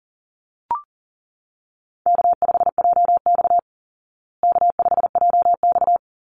Morse code, heard as a keyed tone, about 40 wpm, 700 Hz. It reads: K5JX K5JX
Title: Easy Call Signs Repeated 2x - 40wpm